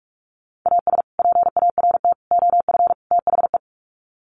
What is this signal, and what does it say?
Morse code, heard as a keyed tone, about 44 wpm, 700 Hz. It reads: AS PART OF THE
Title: Sets of 4 Words 45wpm